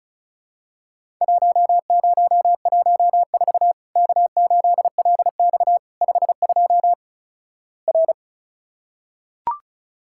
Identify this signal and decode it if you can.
Morse code, heard as a keyed tone, about 36 wpm, 700 Hz. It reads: 1014 K8LX 52 R
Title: Worked All Europe DX Contest - QTC Exchange - 35wpm